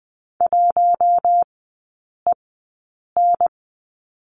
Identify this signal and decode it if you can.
Morse code, heard as a keyed tone, about 20 wpm, spaced as for 8 wpm, 700 Hz. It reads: 1EN
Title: 074 Lesson 13e - Letter L - Three Characters 20wpm